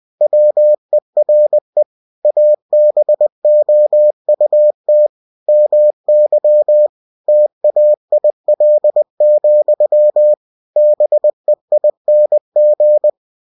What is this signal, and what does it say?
Morse code, heard as a keyed tone, about 20 wpm, 600 Hz. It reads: WERE ABOUT MY TAIL, BEING
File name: cw-books / winnie-the-pooh / winnie-20wpm-page-0089